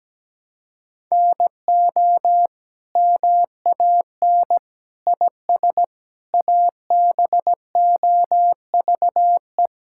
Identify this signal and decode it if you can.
Morse code, heard as a keyed tone, about 17 wpm, 700 Hz. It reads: NO MAN IS ABOVE
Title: Sentences from Top 1000 Words - Review Part 5 - 17wpm